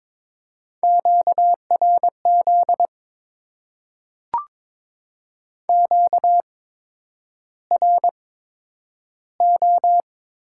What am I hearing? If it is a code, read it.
Morse code, heard as a keyed tone, about 22 wpm, spaced as for 9 wpm, 700 Hz. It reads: QRZ QRO
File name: Top 90 QSO Elements - 8x Character Spacing - 22wpm